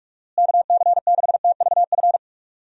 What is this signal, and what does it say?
Morse code, heard as a keyed tone, about 41 wpm, 700 Hz. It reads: KC6TVF
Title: USA Call Signs - 6 Characters - Repeated 2x - 45wpm